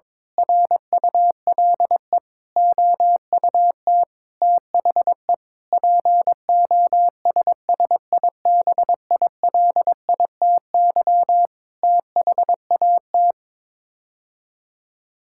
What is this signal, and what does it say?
Morse code, heard as a keyed tone, about 21 wpm, 700 Hz. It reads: RULE OUT THE POSSIBILITY THAT
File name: practice / 22wpm / Sets of 8 Words - 22wpm